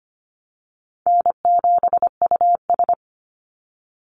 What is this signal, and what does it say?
Morse code, heard as a keyed tone, about 25 wpm, 700 Hz. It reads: N7US
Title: Call Signs Repeated 2x - 25wpm